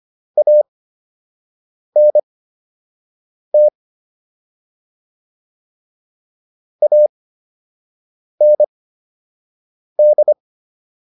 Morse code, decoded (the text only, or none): ANT AND